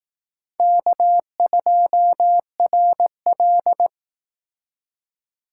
Morse code, heard as a keyed tone, about 18 wpm, 700 Hz. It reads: K2RL
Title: Speed Racing - USA 1x2 Call Signs 15wpm